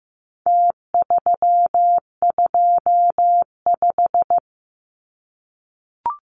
Morse code, heard as a keyed tone, about 15 wpm, 700 Hz. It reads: T325
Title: Numbers - Contest Style - 4-Digit 15wpm